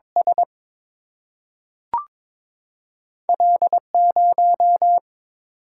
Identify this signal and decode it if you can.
Morse code, heard as a keyed tone, about 22 wpm, 700 Hz. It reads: S L 0
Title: Single-Letter Single-Number 22wpm